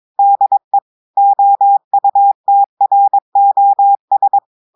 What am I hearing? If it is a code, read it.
Morse code, heard as a keyed tone, about 22 wpm, 800 Hz. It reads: DE OUTROS
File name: audios / portugais / portugais texte en clair 1 vitesse 22 mots minutes